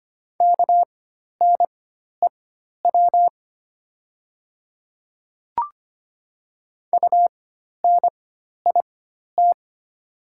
Morse code, heard as a keyed tone, about 25 wpm, spaced as for 12 wpm, 700 Hz. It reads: KNEW UNIT
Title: 4x Character Spacing - Top 500 Words - 25wpm